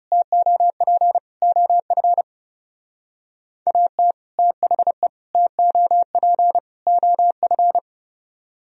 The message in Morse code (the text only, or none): TOP OF AT THE TOP OF